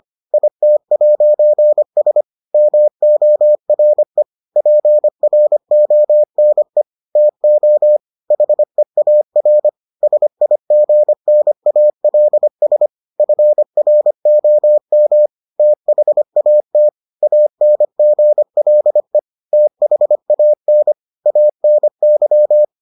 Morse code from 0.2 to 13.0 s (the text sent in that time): IT'S MORE PRONE TO HEAR SIGNALS